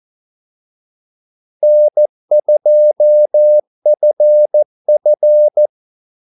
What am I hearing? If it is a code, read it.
Morse code, heard as a keyed tone, about 14 wpm, 600 Hz. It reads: N2FF
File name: Callsigns-14wpm